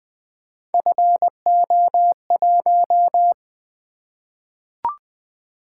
Morse code, heard as a keyed tone, about 20 wpm, 700 Hz. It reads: FO1
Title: Three Letter-Number Combinations - 20wpm